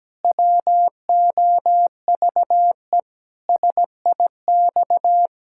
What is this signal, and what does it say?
Morse code, heard as a keyed tone, about 17 wpm, 700 Hz. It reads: WOVE SIX
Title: Pangram Challenge v2 - 17wpm